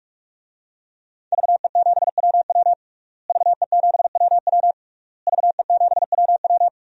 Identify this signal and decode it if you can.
Morse code, heard as a keyed tone, about 46 wpm, 700 Hz. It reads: VE7WW VE7WW VE7WW
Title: DXCC-100 Call Signs - 5 Characters - Repeated 3x - 45wpm